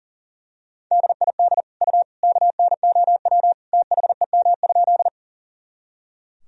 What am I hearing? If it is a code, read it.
Morse code, heard as a keyed tone, about 40 wpm, 700 Hz. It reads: DID U KNOW THEM?
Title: Sentences from Top 100 Words and Abbreviations 40wpm